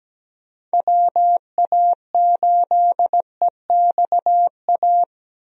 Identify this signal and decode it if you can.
Morse code, heard as a keyed tone, about 17 wpm, 700 Hz. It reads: WA8EXA
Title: USA Call Signs - 6 Characters - Repeated 2x - 17wpm